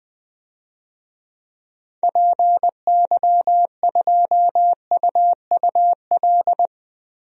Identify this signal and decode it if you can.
Morse code, heard as a keyed tone, about 20 wpm, 700 Hz. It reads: PY2UUL
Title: DXCC-100 Call Signs - 6 Characters - 20wpm